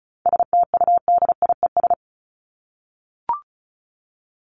Morse code, heard as a keyed tone, about 34 wpm, 700 Hz. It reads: STUDIES
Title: ES:S Suffix - Top 100 Words with Base Words 35wpm